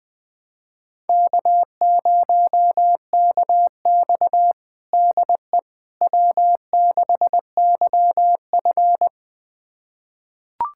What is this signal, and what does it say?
Morse code, heard as a keyed tone, about 20 wpm, 700 Hz. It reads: K0KX DE W6YF
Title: USA Callsign DE Callsign - 4 Characters - 20wpm